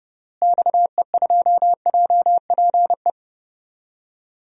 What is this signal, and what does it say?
Morse code, heard as a keyed tone, about 30 wpm, 700 Hz. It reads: XE2JPE
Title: DXCC-100 Call Signs - 6 Characters - Repeated 2x - 30wpm